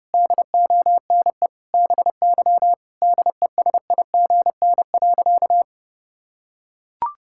Morse code, heard as a keyed tone, about 32 wpm, 700 Hz. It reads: DONE BY DESIGN.
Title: Sentences from Top 1000 Words - Review Part 5 - 30wpm